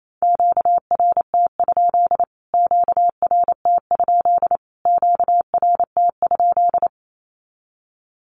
Morse code, heard as a keyed tone, about 28 wpm, 700 Hz. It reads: QRT? QRT? QRT?
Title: Q-Codes Repeated 3x 28wpm